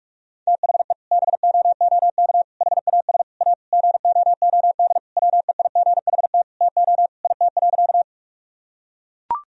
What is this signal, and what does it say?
Morse code, heard as a keyed tone, about 42 wpm, 700 Hz. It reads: THE BOOK HAS A GOOD WEIGHT TO IT.
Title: Sentences from Top 1000 Words - Review Part 3 - 45wpm